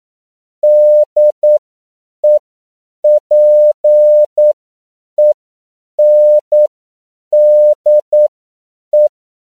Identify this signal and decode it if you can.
Morse code, heard as a keyed tone, about 9 wpm, spaced as for 7 wpm, 600 Hz. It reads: DEPENDE
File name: CW / 10PPM